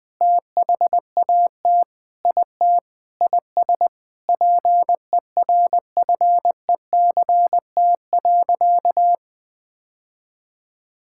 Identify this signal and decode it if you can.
Morse code, heard as a keyed tone, about 19 wpm, 700 Hz. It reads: THAT IT IS PERFECT.
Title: Sentences with 9-Word Phrases - 20wpm